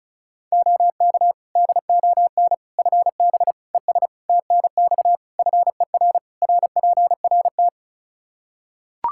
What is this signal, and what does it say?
Morse code, heard as a keyed tone, about 34 wpm, 700 Hz. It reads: OK DON FB ES TNX FER RPRT
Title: Practice QSO - Exchange 2 - Info - 35wpm